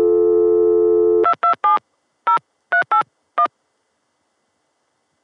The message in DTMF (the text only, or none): 2270381